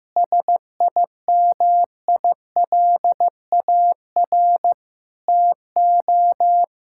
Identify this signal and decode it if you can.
Morse code, heard as a keyed tone, about 16 wpm, 700 Hz. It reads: SIMILAR TO
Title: Sentences from Top 1000 Words - Review Part 5 - 15wpm